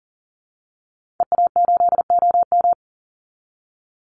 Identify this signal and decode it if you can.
Morse code, heard as a keyed tone, about 40 wpm, 700 Hz. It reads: EA8OM
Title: Call Signs Repeated 2x - 40wpm